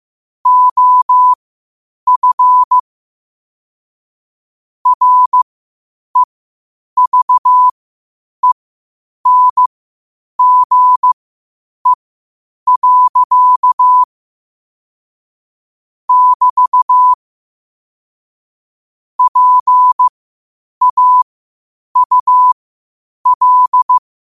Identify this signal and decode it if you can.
Morse code, heard as a keyed tone, about 15 wpm, spaced as for 8 wpm, 1000 Hz. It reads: OF REVENGE. <BT> PAUL
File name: qotdfiles / 8wpm